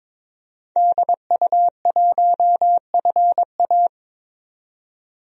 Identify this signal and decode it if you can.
Morse code, heard as a keyed tone, about 22 wpm, 700 Hz. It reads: DU1FA
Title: DXCC-100 Call Signs - 5 Characters - 22wpm